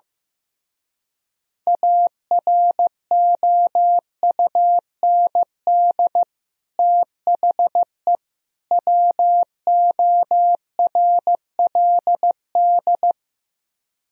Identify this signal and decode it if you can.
Morse code, heard as a keyed tone, about 15 wpm, 700 Hz. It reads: AROUND THE WORLD
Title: Speed-Racing - Sets of 3 Words 15wpm